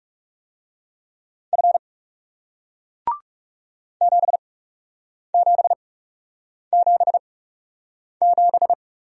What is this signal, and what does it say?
Morse code, heard as a keyed tone, about 39 wpm, spaced as for 8 wpm, 700 Hz. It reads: F 7777